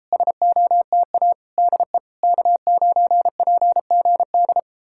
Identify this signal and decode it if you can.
Morse code, heard as a keyed tone, about 38 wpm, 700 Hz. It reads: SOTA DE K9PGD